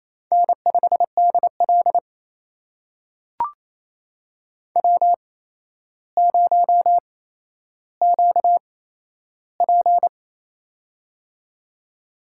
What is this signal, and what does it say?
Morse code, heard as a keyed tone, about 28 wpm, spaced as for 12 wpm, 700 Hz. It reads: N5DL W0QP
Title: USA Call Signs - 4 Characters - 8x Character Spacing - 28wpm